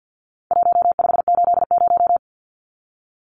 Morse code, heard as a keyed tone, about 50 wpm, 700 Hz. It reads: 1580
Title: Numbers - Contest Style - 4-Digit 50wpm